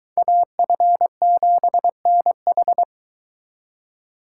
Morse code, heard as a keyed tone, about 23 wpm, 700 Hz. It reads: AF7NH